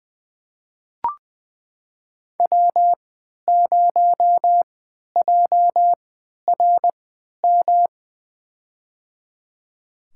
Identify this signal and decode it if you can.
Morse code, heard as a keyed tone, about 20 wpm, spaced as for 11 wpm, 700 Hz. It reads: W0JRM
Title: USA Call Signs - 5 Characters - 3x Character Spacing - 20wpm